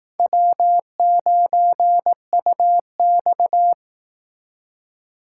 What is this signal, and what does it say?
Morse code, heard as a keyed tone, about 19 wpm, 700 Hz. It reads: W9UX